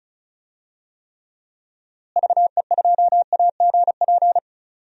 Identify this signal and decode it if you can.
Morse code, heard as a keyed tone, about 34 wpm, 700 Hz. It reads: VE2AGP
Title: DXCC-100 Call Signs - 6 Characters - Repeated 2x - 35wpm